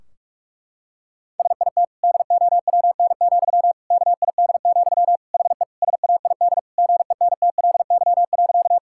Morse code, heard as a keyed tone, about 46 wpm, 700 Hz. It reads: SIT DOWN, KID, HE SAID GENTLY.
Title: Sentences from Top 1600 Words 45wpm